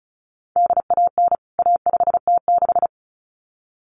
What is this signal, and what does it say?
Morse code, heard as a keyed tone, about 34 wpm, 700 Hz. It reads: DAN A5T6
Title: CWT Exchange 35wpm